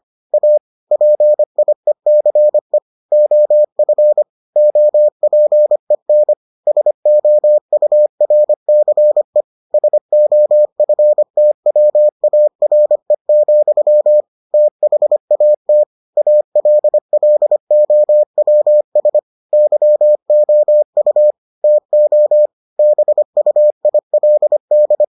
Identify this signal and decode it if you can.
Morse code, heard as a keyed tone, about 25 wpm, 600 Hz. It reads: A PIECE OF OPEN SOURCE SOFTWARE, THAT ALLOWS YOU TO BUILD